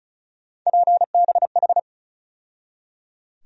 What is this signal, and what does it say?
Morse code, heard as a keyed tone, about 36 wpm, 700 Hz. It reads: PBH
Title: USA Call Sign Postfixes - 3 Letters 35wpm